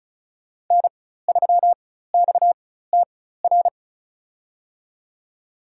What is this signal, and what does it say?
Morse code, heard as a keyed tone, about 34 wpm, spaced as for 16 wpm, 700 Hz. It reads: N3XTR
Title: USA Call Signs - 4x Character Spacing - 35wpm